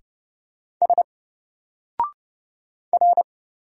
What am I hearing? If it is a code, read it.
Morse code, heard as a keyed tone, about 29 wpm, 700 Hz. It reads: S R
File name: Single-Letter-Number-30wpm